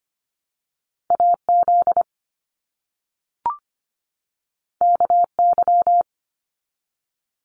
Morse code, heard as a keyed tone, about 25 wpm, 700 Hz. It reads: AZ KY
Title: US States 25wpm